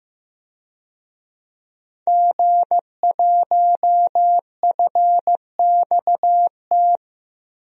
Morse code, heard as a keyed tone, about 15 wpm, 700 Hz. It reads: G1FXT
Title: DXCC-100 Call Signs - 5 Characters - 15wpm